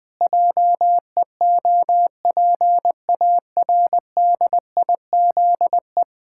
Morse code, heard as a keyed tone, about 20 wpm, 700 Hz. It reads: JEOPARDIZE
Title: Pangram Challenge v2 - 20wpm